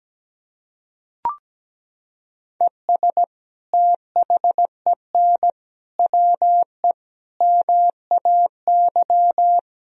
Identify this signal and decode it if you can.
Morse code, heard as a keyed tone, about 17 wpm, 700 Hz. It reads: ES THEN WE MAY
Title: Sentences from Top 300 Words and Abbreviations 17wpm